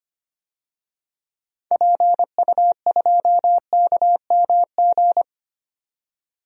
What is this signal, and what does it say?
Morse code, heard as a keyed tone, about 25 wpm, 700 Hz. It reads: PU2KMG
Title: DXCC-100 Call Signs - 6 Characters - 25wpm